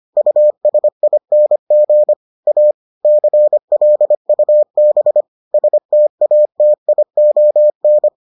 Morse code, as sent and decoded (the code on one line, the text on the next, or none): ..- ... .. -. --. / .- / -.-. .-.. ..- -... / ... - .- - .. --- -.
USING A CLUB STATION